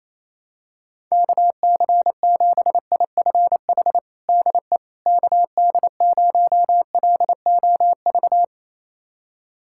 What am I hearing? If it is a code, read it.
Morse code, heard as a keyed tone, about 28 wpm, 700 Hz. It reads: KC7IFH DE KD0LOV
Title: USA Callsign DE Callsign 28wpm